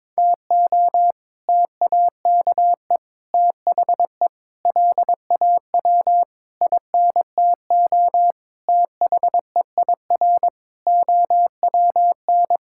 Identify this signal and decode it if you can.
Morse code, heard as a keyed tone, about 21 wpm, 700 Hz. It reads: TO TAKE THE LAW INTO THEIR OWN
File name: practice / 22wpm / Sets of 8 Words - 22wpm